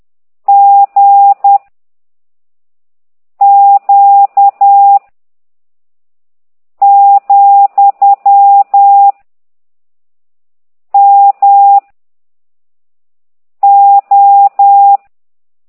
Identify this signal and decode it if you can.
Morse code, heard as a keyed tone, about 10 wpm, 800 Hz. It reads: G Q , M O
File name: ct2idw / exerc11